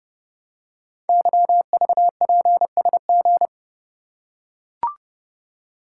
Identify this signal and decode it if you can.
Morse code, heard as a keyed tone, about 29 wpm, 700 Hz. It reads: YVPSG
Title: Code Groups - 5 Letters - 30wpm